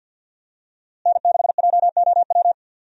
Morse code, heard as a keyed tone, about 51 wpm, 700 Hz. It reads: N6JOW
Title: USA Call Signs - 5 Characters - Repeated 2x - 50wpm